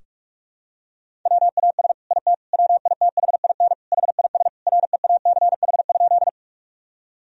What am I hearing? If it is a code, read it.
Morse code, heard as a keyed tone, about 45 wpm, 700 Hz. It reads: WAS IT WITHIN HIS REACH?
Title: Sentences from Top 1000 Words - Review Part 4 - 45wpm